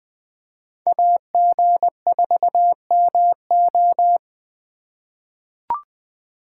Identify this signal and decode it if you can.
Morse code, heard as a keyed tone, about 20 wpm, 700 Hz. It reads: AG4MO
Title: USA Call Signs - 5 Characters - 8x Character Spacing - 20wpm